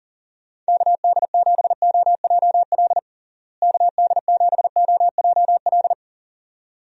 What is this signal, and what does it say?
Morse code, heard as a keyed tone, about 40 wpm, 700 Hz. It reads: KD7OJL KD7OJL
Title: USA Call Signs - 6 Characters - Repeated 2x - 40wpm